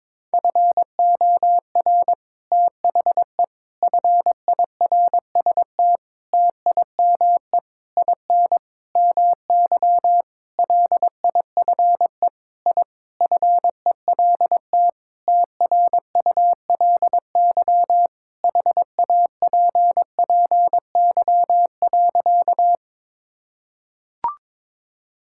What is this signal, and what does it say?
Morse code, heard as a keyed tone, about 21 wpm, 700 Hz. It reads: FOR THE FIRST TIME IN MY LIFE I FELT TRULY HAPPY.
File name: Sentences with 7-Word Phrases - 22wpm